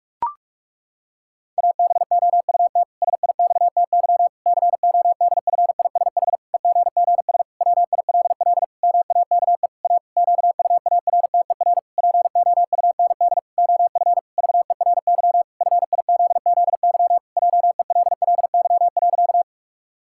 Morse code, heard as a keyed tone, about 46 wpm, 700 Hz. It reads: ABOUT SIXTY CODFISH EGGS WILL MAKE A QUARTER POUND OF VERY FIZZY JELLY.
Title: Pangram Challenge v2 - 45wpm